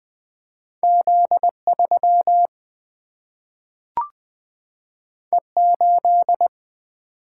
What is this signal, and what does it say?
Morse code, heard as a keyed tone, about 20 wpm, 700 Hz. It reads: Z 3 E 8
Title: Single-Letter Single-Number 20wpm